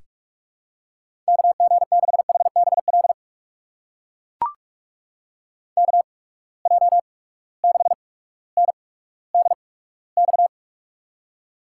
Morse code, heard as a keyed tone, about 44 wpm, spaced as for 19 wpm, 700 Hz. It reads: KG6HBL KJ6NDX
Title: USA Call Signs - 6 Characters - 8x Character Spacing - 45wpm